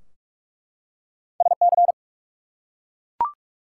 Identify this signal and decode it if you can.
Morse code, heard as a keyed tone, about 44 wpm, 700 Hz. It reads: SC
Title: US States and Canadian Provinces 45wpm